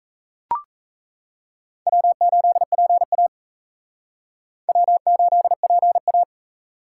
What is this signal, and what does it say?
Morse code, heard as a keyed tone, about 36 wpm, 700 Hz. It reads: W8PA W8PA